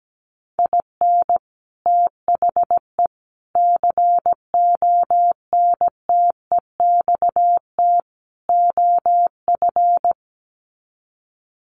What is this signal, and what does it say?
Morse code, heard as a keyed tone, about 17 wpm, 700 Hz. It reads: IN THE CONTEXT OF
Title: Sets of 4 Words 17wpm